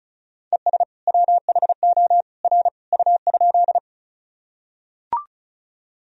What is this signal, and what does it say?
Morse code, heard as a keyed tone, about 36 wpm, 700 Hz. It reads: ES WHO R U?
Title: Sentences from Top 100 Words and Abbreviations 35wpm